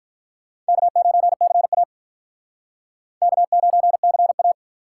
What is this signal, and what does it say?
Morse code, heard as a keyed tone, about 43 wpm, 700 Hz. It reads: K9CA K9CA
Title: Speed Racing - USA Call Signs 35wpm